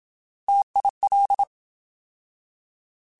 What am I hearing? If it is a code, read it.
Morse code, heard as a keyed tone, about 27 wpm, 800 Hz. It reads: TIL
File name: Alphabetical / Alphabetical mp3s / Ts @25